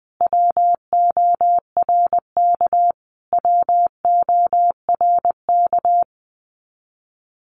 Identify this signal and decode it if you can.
Morse code, heard as a keyed tone, about 20 wpm, 700 Hz. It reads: WORK WORK
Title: Top 500 Words Repeated 20wpm